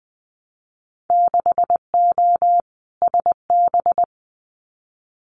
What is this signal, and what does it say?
Morse code, heard as a keyed tone, about 20 wpm, 700 Hz. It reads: 6O SB
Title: Winter Field Day Exchanges Encore 20wpm